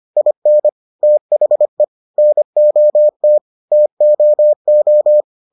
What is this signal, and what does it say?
Morse code, heard as a keyed tone, about 25 wpm, 600 Hz. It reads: IN THE NOT TOO